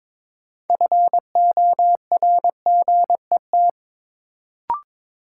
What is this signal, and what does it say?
Morse code, heard as a keyed tone, about 22 wpm, 700 Hz. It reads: FORGET
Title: Sentences from Top 1200 Words - New Word Drill Alt - 22wpm